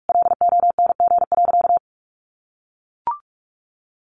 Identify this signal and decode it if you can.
Morse code, heard as a keyed tone, about 44 wpm, 700 Hz. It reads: LONG.
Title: Sentences from Top 1300 Words 45wpm